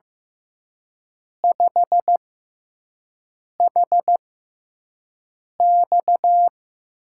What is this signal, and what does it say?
Morse code, heard as a keyed tone, about 15 wpm, spaced as for 5 wpm, 700 Hz. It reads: 5HX
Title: USA Call Signs - 4 Characters - 6x Character Spacing - 15wpm